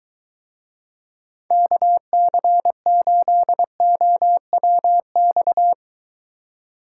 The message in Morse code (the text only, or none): KC8OWX